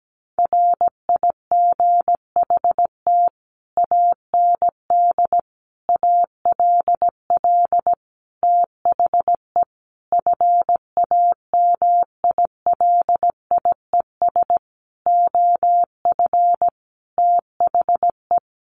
Morse code, heard as a keyed tone, about 17 wpm, 700 Hz. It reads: RIGHT AND ALL THE FAMILIES OF THE
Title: A Sound Of Thunder 17wpm